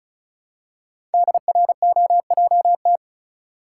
Morse code, heard as a keyed tone, about 34 wpm, 700 Hz. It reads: DROJT
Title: Code Groups - 5 Letters - 35wpm